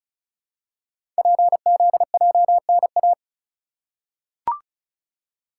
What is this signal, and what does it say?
Morse code, heard as a keyed tone, about 36 wpm, 700 Hz. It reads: PZJNA
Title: Code Groups - 5 Letters - 35wpm